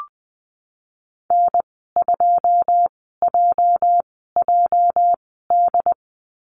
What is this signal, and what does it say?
Morse code, heard as a keyed tone, about 20 wpm, spaced as for 15 wpm, 700 Hz. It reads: N2JJD
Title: 219 Lesson 35g - Number 0 - Call Signs 20wpm